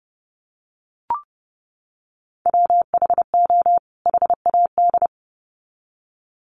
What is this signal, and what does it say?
Morse code, heard as a keyed tone, about 30 wpm, 700 Hz. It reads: WHO HAD
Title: Sets of 2 Words 30wpm